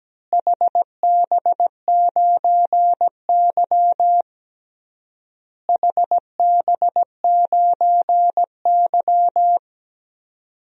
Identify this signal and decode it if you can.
Morse code, heard as a keyed tone, about 17 wpm, 700 Hz. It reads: HB9Y HB9Y
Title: DXCC-100 Call Signs - 4 Characters - Repeated 3x - 17wpm